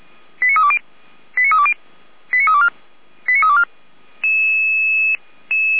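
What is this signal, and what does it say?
Five-tone selective call (ZVEI), heard as quick runs of five tones, tones of about 70 ms. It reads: 89320 89320 89324 89324
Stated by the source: ZVEI five-tone call received by SDR, archived by feuerwehr-orlamuende.org (fire brigade Orlamünde).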